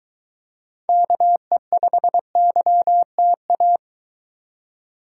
Morse code, heard as a keyed tone, about 23 wpm, 700 Hz. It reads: KE5YTA